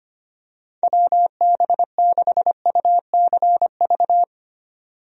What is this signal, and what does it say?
Morse code, heard as a keyed tone, about 25 wpm, 700 Hz. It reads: WB6UCV
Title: USA Call Signs - 6 Characters - 25wpm